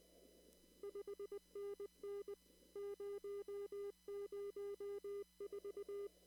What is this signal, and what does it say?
Morse code, heard as a keyed tone, about 19 wpm, 400 Hz. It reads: 5NN 004
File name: CW / kt5j wpx cw-op-k5pi